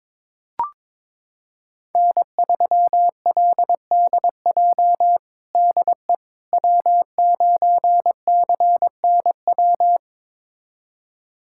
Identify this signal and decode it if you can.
Morse code, heard as a keyed tone, about 22 wpm, 700 Hz. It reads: N3LDJ DE W9CNW
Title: USA Callsign DE Callsign - 5 Characters - 22wpm